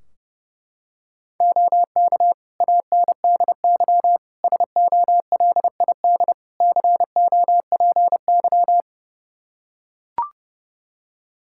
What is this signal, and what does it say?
Morse code, heard as a keyed tone, about 30 wpm, 700 Hz. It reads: OK ANDY SOLID COPY
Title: K1USN SST QSO Practice 30wpm